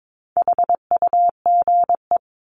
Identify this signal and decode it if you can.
Morse code, heard as a keyed tone, about 22 wpm, 700 Hz. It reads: HUGE
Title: Top 1001-1250 Words 22wpm